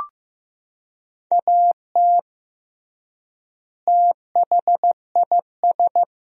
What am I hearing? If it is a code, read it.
Morse code, heard as a keyed tone, about 14 wpm, 700 Hz. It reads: AT THIS
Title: 3x Word-Spacing - Sentences from Top 300 Words - 15wpm